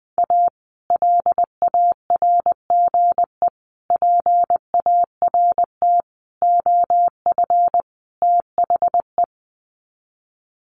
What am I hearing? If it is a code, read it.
Morse code, heard as a keyed tone, about 20 wpm, 700 Hz. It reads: A LARGE PART OF THE
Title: Sets of 5 Words Encore2 20wpm